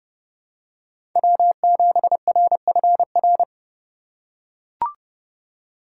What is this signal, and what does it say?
Morse code, heard as a keyed tone, about 32 wpm, 700 Hz. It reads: W7RFR
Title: USA Call Signs - 5 Characters - 4x Character Spacing - 30wpm